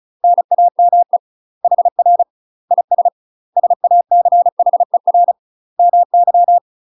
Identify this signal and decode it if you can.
Morse code, heard as a keyed tone, about 36 wpm, 700 Hz. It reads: NAME HR IS SACHER MY
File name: UK MASTER QSOs TRAINING / 35WPM / UK MASTER QSOs 0031